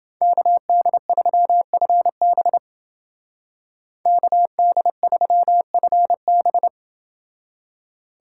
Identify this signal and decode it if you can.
Morse code, heard as a keyed tone, about 30 wpm, 700 Hz. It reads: KD3FB KD3FB